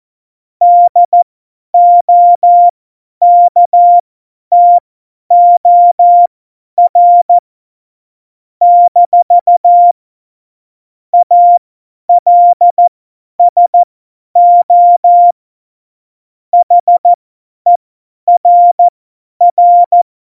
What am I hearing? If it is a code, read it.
Morse code, heard as a keyed tone, about 14 wpm, spaced as for 10 wpm, 700 Hz. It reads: DOKTOR - ALSO HERR
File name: Morsekurs / Witze / Witz 13. Oktober 2020, 50 F70